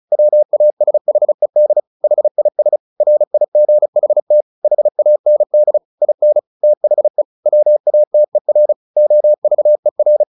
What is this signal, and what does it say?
Morse code, heard as a keyed tone, about 34 wpm, 600 Hz. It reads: WASHED HIS RIGHT HAND IN THE WATER OVER
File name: the old man and the sea 2 35wpm